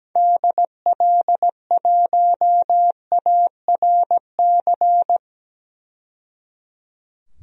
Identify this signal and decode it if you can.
Morse code, heard as a keyed tone, about 17 wpm, 700 Hz. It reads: DL1ARC
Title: DXCC-100 Call Signs - 6 Characters - 17wpm